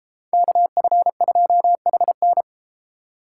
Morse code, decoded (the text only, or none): KF2HN